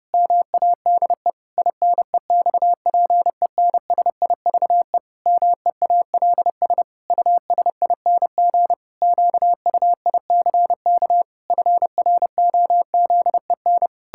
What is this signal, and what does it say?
Morse code, heard as a keyed tone, about 30 wpm, 700 Hz. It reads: MADE INEXPENSIVE MEALS USING QUICK FROZEN
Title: Pangram Challenge v2 - 30wpm